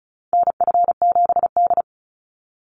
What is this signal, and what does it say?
Morse code, heard as a keyed tone, about 36 wpm, 700 Hz. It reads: NF7D
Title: Call Signs - 35wpm